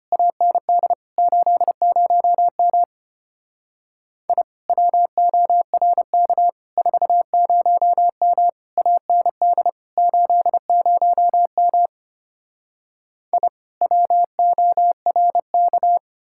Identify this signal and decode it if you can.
Morse code, heard as a keyed tone, about 32 wpm, 700 Hz. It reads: AND 80M I WORK 40M AND 80M I WORK